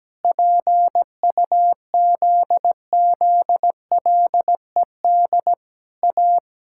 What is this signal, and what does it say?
Morse code, heard as a keyed tone, about 17 wpm, 700 Hz. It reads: PUZZLED A
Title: Pangram Challenge v2 - 17wpm